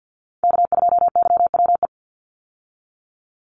Morse code, heard as a keyed tone, about 50 wpm, 700 Hz. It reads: K2YWE
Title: Call Signs Repeated 2x - 50wpm